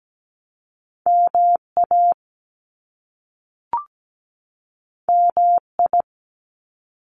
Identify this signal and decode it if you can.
Morse code, heard as a keyed tone, about 17 wpm, 700 Hz. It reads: MA MI
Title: US States 17wpm